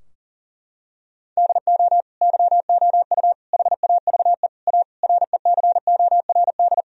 Morse code, heard as a keyed tone, about 40 wpm, 700 Hz. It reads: DO YOU HAVE A RECORD
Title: Sentences from Top 1000 Words - Review Part 3 - 40wpm